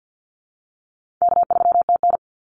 Morse code, heard as a keyed tone, about 50 wpm, 700 Hz. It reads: K3TN
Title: Call Signs - 50wpm